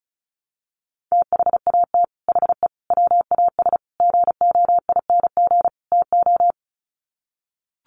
Morse code, heard as a keyed tone, about 34 wpm, 700 Hz. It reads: THAT HE WAS GOING TO
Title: Sets of 5 Words 35wpm